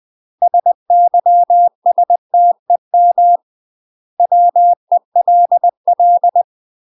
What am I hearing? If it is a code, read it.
Morse code, heard as a keyed tone, about 20 wpm, 700 Hz. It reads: SYSTEM WELL
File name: UK MASTER QSOs TRAINING / 20WPM / UK MASTER QSOs 0039